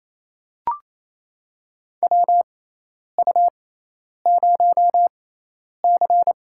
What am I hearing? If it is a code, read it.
Morse code, heard as a keyed tone, about 28 wpm, spaced as for 10 wpm, 700 Hz. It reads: WU0C
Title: USA Call Signs - 4 Characters - 6x Character Spacing - 28wpm